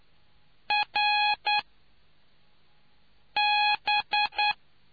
Morse code, heard as a keyed tone, about 9 wpm, 830 Hz. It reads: R B